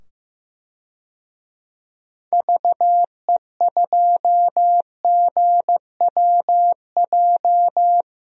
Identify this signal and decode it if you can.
Morse code, heard as a keyed tone, about 15 wpm, 700 Hz. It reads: VE2GWJ
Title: DXCC-100 Call Signs - 6 Characters - 15wpm